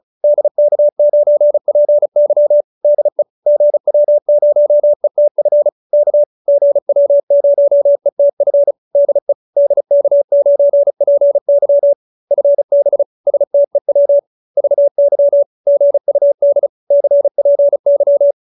Morse code, heard as a keyed tone, about 34 wpm, 570 Hz. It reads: DK9PY DE GW0ETF K GW0ETF DE DK9PY FB STEW VY GUD CPY